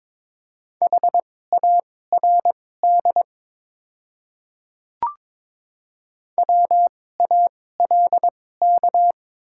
Morse code, heard as a keyed tone, about 22 wpm, spaced as for 16 wpm, 690 Hz. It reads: HARD WALK
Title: 2x Character Spacing - Top 500 Words - 22wpm